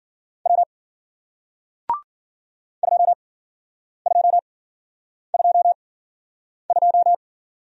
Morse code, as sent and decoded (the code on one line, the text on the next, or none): .-- / ..--- ..--- ..--- ..---
W 2222